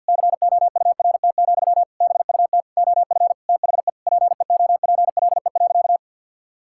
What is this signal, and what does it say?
Morse code, heard as a keyed tone, about 50 wpm, 700 Hz. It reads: COURT, BUT OF THE PEOPLE.
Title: Sentences from Top 1600 Words 50wpm